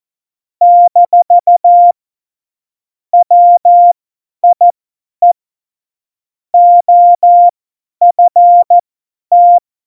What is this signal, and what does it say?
Morse code, heard as a keyed tone, about 14 wpm, spaced as for 10 wpm, 700 Hz. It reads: - WIE OFT
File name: Morsekurs / Witze / Witz 11 August 2020, 50 F70